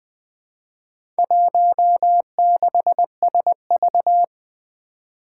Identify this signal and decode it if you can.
Morse code, heard as a keyed tone, about 20 wpm, 700 Hz. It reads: 16SV
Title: Four Challenging Letter-Number Combinations 20wpm